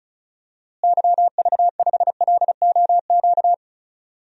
Morse code, heard as a keyed tone, about 36 wpm, 700 Hz. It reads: YV5LOQ
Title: DXCC-100 Call Signs - 6 Characters - Repeated 2x - 35wpm